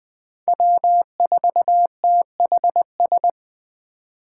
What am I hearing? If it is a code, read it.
Morse code, heard as a keyed tone, about 20 wpm, 700 Hz. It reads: W4THS
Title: Speed Racing - USA 1x3 Call Signs 17wpm